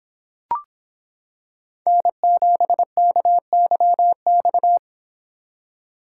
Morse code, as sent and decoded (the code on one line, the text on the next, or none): -. --... -.- -.-- -..-
N7KYX